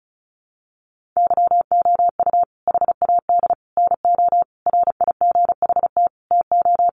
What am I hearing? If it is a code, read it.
Morse code, heard as a keyed tone, about 34 wpm, 700 Hz. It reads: YOU HAD NO RIGHT TO
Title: Sentences from Top 1300 Words 35wpm